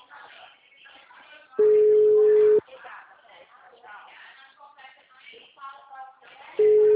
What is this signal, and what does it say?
Ringback tone (Europe, 425 Hz), with speech.